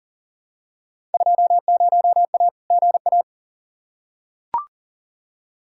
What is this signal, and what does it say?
Morse code, heard as a keyed tone, about 40 wpm, 700 Hz. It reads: 20A GA
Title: Field Day Exchanges Encore - 40wpm